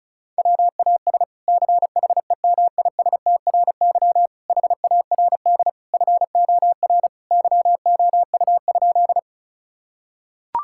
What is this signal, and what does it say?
Morse code, heard as a keyed tone, about 34 wpm, 700 Hz. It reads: WAS CHEMISTRY HARD FOR YOU?
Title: Sentences from Top 1000 Words - Review Part 6 - 35wpm